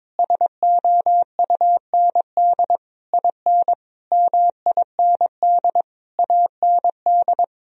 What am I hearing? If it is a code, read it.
Morse code, heard as a keyed tone, about 22 wpm, 700 Hz. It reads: SOUND IN MIND AND
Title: Sentences from Top 1000 Words - Review Part 2 - 22wpm